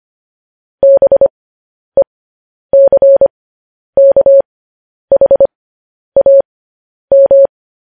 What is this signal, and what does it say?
Morse code, heard as a keyed tone, about 25 wpm, spaced as for 10 wpm, 550 Hz. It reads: BECKHAM
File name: podcast / News headlines-1x-25x10